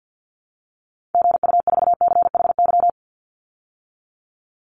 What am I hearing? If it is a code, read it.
Morse code, heard as a keyed tone, about 49 wpm, 700 Hz. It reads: GU4CHY
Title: Call Signs - 50wpm